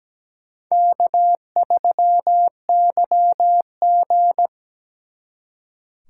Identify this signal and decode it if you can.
Morse code, heard as a keyed tone, about 20 wpm, 700 Hz. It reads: K3YG
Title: Speed Racing - USA 1x2 Call Signs 17wpm